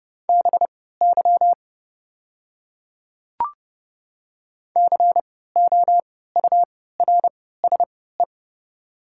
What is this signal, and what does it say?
Morse code, heard as a keyed tone, about 30 wpm, spaced as for 17 wpm, 700 Hz. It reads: BY COURSE
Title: 3x Character Spacing - Top 500 Words - 30wpm